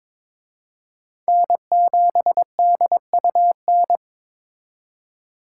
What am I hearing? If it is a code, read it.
Morse code, heard as a keyed tone, about 26 wpm, 700 Hz. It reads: N7DUN
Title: Speed Racing - USA 1x3 Call Signs 22wpm